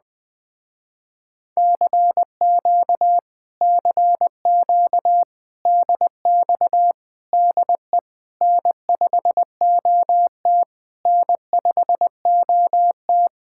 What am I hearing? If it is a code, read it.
Morse code, heard as a keyed tone, about 20 wpm, 700 Hz. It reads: CQ CQ DX DE N5OT N5OT
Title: Speed-Racing Common QSO Phrases - 20wpm